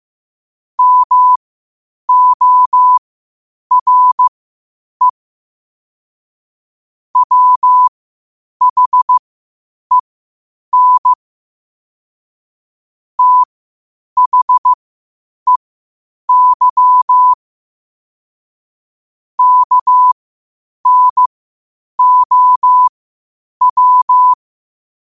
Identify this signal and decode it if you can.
Morse code, heard as a keyed tone, about 15 wpm, spaced as for 8 wpm, 1000 Hz. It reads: MORE WHEN THEY KNOW
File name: qotdfiles / 8wpm